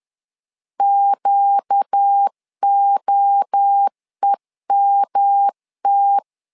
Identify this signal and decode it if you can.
Morse code, heard as a keyed tone, about 10 wpm, 800 Hz. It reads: QOEMT
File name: LEZIONI / 4 Q 10wpm 2210